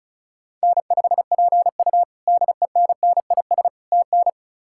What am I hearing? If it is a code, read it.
Morse code, heard as a keyed tone, about 36 wpm, 700 Hz. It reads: N5PU DENNIS TN
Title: K1USN SST QSO Practice - 35wpm